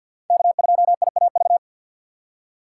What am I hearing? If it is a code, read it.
Morse code, heard as a keyed tone, about 51 wpm, 700 Hz. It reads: K2IAV
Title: USA Call Signs - 5 Characters - 2x Character Spacing - 50wpm